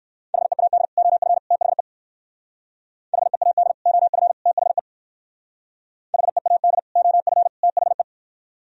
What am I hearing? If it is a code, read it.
Morse code, heard as a keyed tone, about 60 wpm, 700 Hz. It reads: HEAD OF THE HEAD OF THE HEAD OF THE